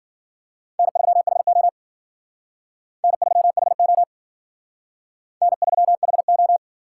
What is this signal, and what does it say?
Morse code, heard as a keyed tone, about 52 wpm, 700 Hz. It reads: N3HO N3HO N3HO